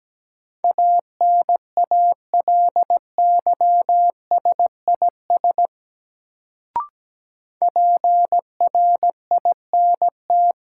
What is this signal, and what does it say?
Morse code, heard as a keyed tone, about 17 wpm, 700 Hz. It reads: ANALYSIS PRINT
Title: Sentences from Top 1400 Words - New Word Drill Alt - 17wpm